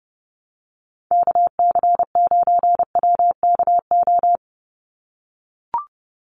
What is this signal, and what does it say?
Morse code, heard as a keyed tone, about 30 wpm, 700 Hz. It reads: KC9WKO
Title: USA Call Signs 30wpm